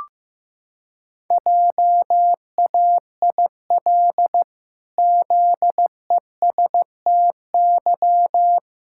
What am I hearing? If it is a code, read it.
Morse code, heard as a keyed tone, about 15 wpm, 700 Hz. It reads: JAIL ZESTY
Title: Pangram Challenge v2 - 15wpm